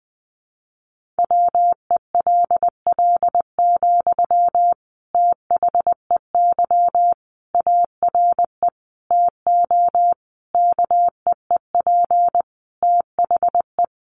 Morse code, heard as a keyed tone, about 20 wpm, 700 Hz. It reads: WELL, THEY ARE TO KEEP THE
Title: Sentences from Top 1500 Words 20wpm